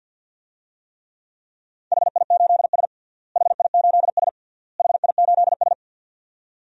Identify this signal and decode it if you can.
Morse code, heard as a keyed tone, about 48 wpm, 700 Hz. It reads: HI8S HI8S HI8S
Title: DXCC-100 Call Signs - 4 Characters - Repeated 3x - 50wpm